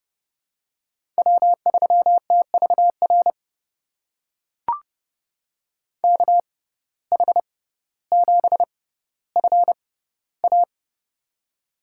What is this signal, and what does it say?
Morse code, heard as a keyed tone, about 31 wpm, spaced as for 15 wpm, 700 Hz. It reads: W3TVR KH7FA
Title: USA Call Signs - 5 Characters - 6x Character Spacing - 30wpm